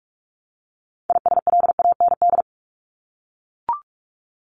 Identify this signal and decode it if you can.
Morse code, heard as a keyed tone, about 44 wpm, 700 Hz. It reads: ISLAND
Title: Top 500 Words 45wpm